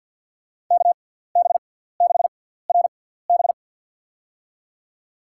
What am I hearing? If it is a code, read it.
Morse code, heard as a keyed tone, about 51 wpm, spaced as for 17 wpm, 700 Hz. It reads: KB6RB
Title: USA Call Signs - 6x Character Spacing - 50wpm